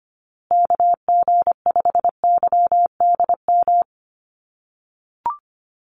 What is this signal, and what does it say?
Morse code, heard as a keyed tone, about 25 wpm, 700 Hz. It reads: KG5YDM
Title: USA Call Signs Repeated 3x 25wpm